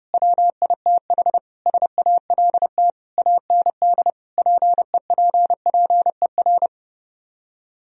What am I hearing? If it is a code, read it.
Morse code, heard as a keyed tone, about 33 wpm, 700 Hz. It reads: WITH SALT AND PEPPER